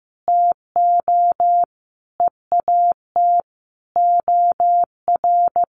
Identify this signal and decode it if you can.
Morse code, heard as a keyed tone, about 15 wpm, 700 Hz. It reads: TO EAT OR
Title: Sentences from Top 1500 Words 15wpm